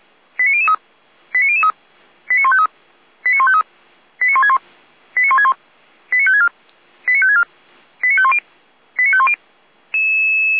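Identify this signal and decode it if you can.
Five-tone selective call (ZVEI), heard as quick runs of five tones, tones of about 70 ms. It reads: 89003 89003 89153 89153 89161 89161 89564 89564 89510 89510
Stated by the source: ZVEI five-tone selective call received by SDR, code 89168.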